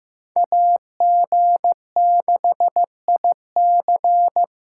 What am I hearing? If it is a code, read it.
Morse code, heard as a keyed tone, about 18 wpm, 700 Hz. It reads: AG6IC
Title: Speed Racing - USA 2x2 Call Signs 15wpm